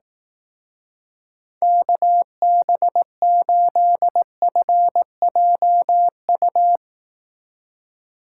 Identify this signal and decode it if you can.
Morse code, heard as a keyed tone, about 18 wpm, 700 Hz. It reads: KB8FJU
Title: Speed Racing - USA 2x3 Call Signs 15wpm